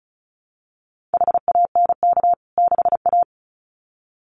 Morse code, heard as a keyed tone, about 34 wpm, 700 Hz. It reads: HANK 6A
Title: CWT Exchange 35wpm